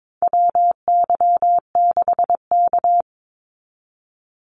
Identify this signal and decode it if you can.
Morse code, heard as a keyed tone, about 22 wpm, 700 Hz. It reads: WY6K
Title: Call Signs - 22wpm